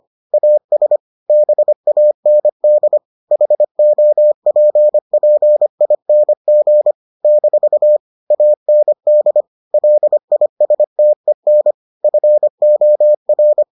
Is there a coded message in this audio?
Yes: Morse code from 0.3 to 13.6 s, heard as a keyed tone, about 25 wpm, 600 Hz. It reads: AS BAND HOPPING - AND LISTEN FOR